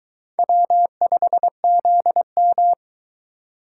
Morse code, heard as a keyed tone, about 22 wpm, 700 Hz. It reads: W5ZM